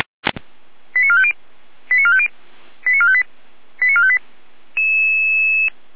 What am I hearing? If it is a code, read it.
Five-tone selective call (ZVEI), heard as quick runs of five tones, tones of about 70 ms. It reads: 89450 89450 89458 89458